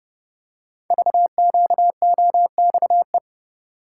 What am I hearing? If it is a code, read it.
Morse code, heard as a keyed tone, about 30 wpm, 700 Hz. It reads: VQOXE
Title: Code Groups - 5 Letters - 30wpm